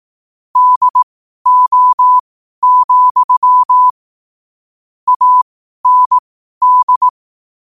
Morse code, heard as a keyed tone, about 18 wpm, spaced as for 12 wpm, 1000 Hz. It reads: DO, AND
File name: qotdfiles / 12wpm